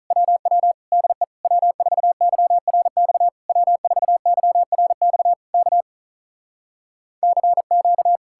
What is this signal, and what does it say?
Morse code, heard as a keyed tone, about 46 wpm, 700 Hz. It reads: WW DE W4YRX W4YRX K CQ